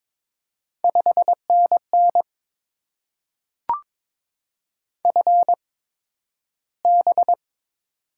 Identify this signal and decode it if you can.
Morse code, heard as a keyed tone, about 22 wpm, spaced as for 12 wpm, 700 Hz. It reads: 5NN FB
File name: Top 20 QSO Elements - 8x Character Spacing - 22wpm